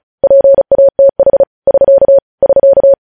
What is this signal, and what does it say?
Morse code, heard as a keyed tone, about 36 wpm, 550 Hz. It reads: PATH <SK> <SK>
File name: podcast / News headlines-1x-35x35